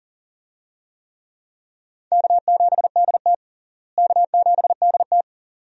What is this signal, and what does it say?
Morse code, heard as a keyed tone, about 37 wpm, 700 Hz. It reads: K7DT K7DT
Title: USA Call Signs - 4 Characters - Repeated 2x - 40wpm